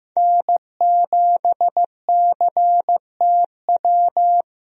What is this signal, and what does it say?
Morse code, heard as a keyed tone, about 15 wpm, 700 Hz. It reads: N7CTW
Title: USA Call Signs - 5 Characters - Repeated 3x - 15wpm